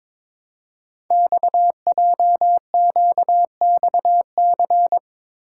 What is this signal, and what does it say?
Morse code, heard as a keyed tone, about 22 wpm, 700 Hz. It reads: XJQXC
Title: Code Groups - 5 Letters - 22wpm